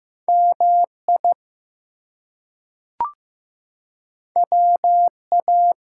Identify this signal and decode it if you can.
Morse code, heard as a keyed tone, about 15 wpm, 700 Hz. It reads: MI WA
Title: US States and Canadian Provinces 15wpm